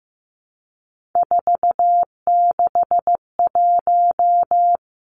Morse code, heard as a keyed tone, about 15 wpm, 700 Hz. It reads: 461
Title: Three Challenging Letter-Number Combinations 15wpm